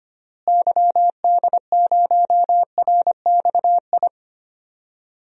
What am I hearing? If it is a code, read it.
Morse code, heard as a keyed tone, about 25 wpm, 700 Hz. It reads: YD0RXI
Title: DXCC-100 Call Signs - 6 Characters - Repeated 2x - 25wpm